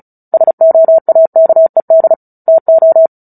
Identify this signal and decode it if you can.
Morse code, heard as a keyed tone, about 37 wpm, 650 Hz. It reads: SOAKED TO
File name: Book Two / 09-Chapter 9 - 35 WPM